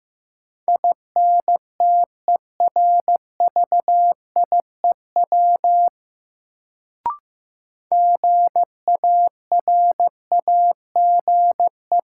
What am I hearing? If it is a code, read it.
Morse code, heard as a keyed tone, about 15 wpm, 700 Hz. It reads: INTERVIEW GARAGE
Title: Sentences from Top 1600 Words - New Word Drill Alt - 15wpm